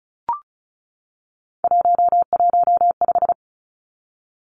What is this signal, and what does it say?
Morse code, heard as a keyed tone, about 34 wpm, 700 Hz. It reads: 115
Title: Three Challenging Letter-Number Combinations 35wpm